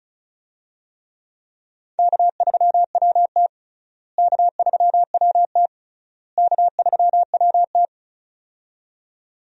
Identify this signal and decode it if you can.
Morse code, heard as a keyed tone, about 34 wpm, 700 Hz. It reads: K3WT K3WT K3WT
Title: USA Call Signs - 4 Characters - Repeated 3x - 35wpm